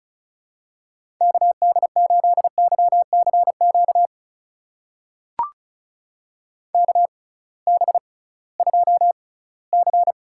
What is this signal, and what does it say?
Morse code, heard as a keyed tone, about 36 wpm, spaced as for 20 wpm, 700 Hz. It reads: KD8YCQ KB2C
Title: USA Call Signs - 6 Characters - 6x Character Spacing - 35wpm